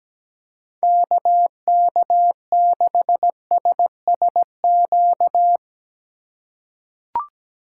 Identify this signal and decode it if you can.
Morse code, heard as a keyed tone, about 17 wpm, 700 Hz. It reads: KK6SSQ
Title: USA Call Signs - 6 Characters - 4x Character Spacing - 17wpm